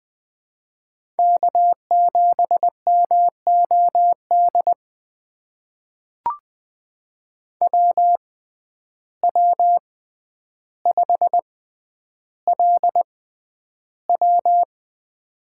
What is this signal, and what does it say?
Morse code, heard as a keyed tone, about 20 wpm, spaced as for 10 wpm, 700 Hz. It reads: K7MOD WW5LW
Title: USA Call Signs - 6x Character Spacing - 20wpm